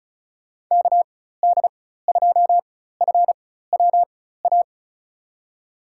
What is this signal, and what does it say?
Morse code, heard as a keyed tone, about 36 wpm, spaced as for 16 wpm, 700 Hz. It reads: KD2FWA
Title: USA Call Signs - 4x Character Spacing - 35wpm